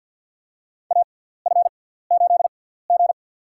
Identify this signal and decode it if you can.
Morse code, heard as a keyed tone, about 51 wpm, spaced as for 17 wpm, 700 Hz. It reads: AF8G
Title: USA Call Signs - 4 Characters - 6x Character Spacing - 50wpm